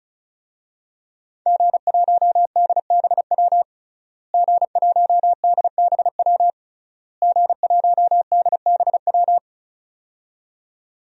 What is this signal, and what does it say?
Morse code, heard as a keyed tone, about 36 wpm, 700 Hz. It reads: G1DBW G1DBW G1DBW
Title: DXCC-100 Call Signs - 5 Characters - Repeated 3x - 35wpm